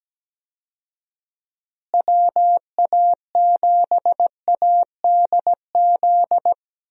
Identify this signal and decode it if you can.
Morse code, heard as a keyed tone, about 17 wpm, 700 Hz. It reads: WA7ADZ
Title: USA Call Signs - 6 Characters - Repeated 2x - 17wpm